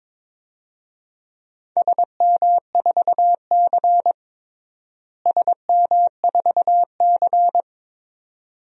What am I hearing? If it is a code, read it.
Morse code, heard as a keyed tone, about 21 wpm, 700 Hz. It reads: SM4C SM4C
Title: DXCC-100 Call Signs - 4 Characters - Repeated 2x - 22wpm